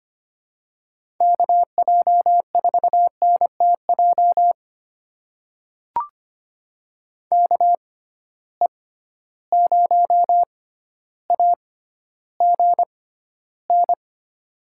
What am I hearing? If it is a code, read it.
Morse code, heard as a keyed tone, about 25 wpm, spaced as for 13 wpm, 700 Hz. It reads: KJ4NTJ KE0AGN
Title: USA Call Signs - 6 Characters - 6x Character Spacing - 25wpm